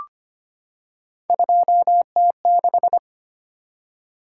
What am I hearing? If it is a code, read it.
Morse code, heard as a keyed tone, about 25 wpm, 700 Hz. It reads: 2T6
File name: Three Letter-Number Combinations - 25wpm